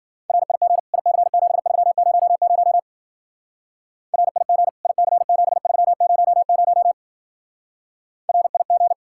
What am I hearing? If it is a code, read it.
Morse code, heard as a keyed tone, about 47 wpm, 700 Hz. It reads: RIG IC7300 RIG IC7300 RIG
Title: Speed-Racing Common QSO Phrases Encore - 40wpm